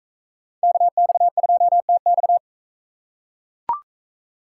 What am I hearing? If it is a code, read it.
Morse code, heard as a keyed tone, about 42 wpm, 700 Hz. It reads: KX2TX